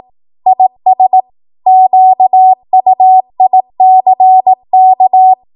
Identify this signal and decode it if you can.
Morse code, heard as a keyed tone, about 18 wpm, 750 Hz. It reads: IS QUICK
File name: Archive / 18 WPM / 18WPM